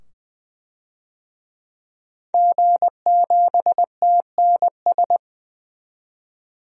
Morse code, heard as a keyed tone, about 20 wpm, 700 Hz. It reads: G7TNS
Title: DXCC-100 Call Signs - 5 Characters - 20wpm